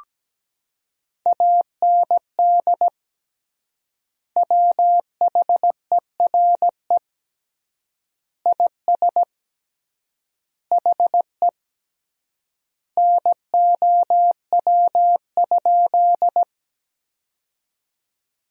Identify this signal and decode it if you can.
Morse code, heard as a keyed tone, about 17 wpm, 700 Hz. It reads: AND WHERE IS HE NOW?
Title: 3x Word-Spacing - Sentences from Top 200 Words - 17wpm